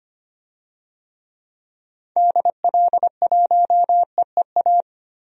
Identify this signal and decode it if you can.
Morse code, heard as a keyed tone, about 25 wpm, 700 Hz. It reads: DL1EEA
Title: DXCC-100 Call Signs - 6 Characters - 25wpm